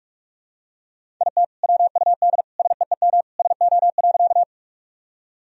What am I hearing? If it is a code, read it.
Morse code, heard as a keyed tone, about 47 wpm, 700 Hz. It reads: IT WUD SEEM SO.
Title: Sentences from Top 300 Words and Abbreviations 45wpm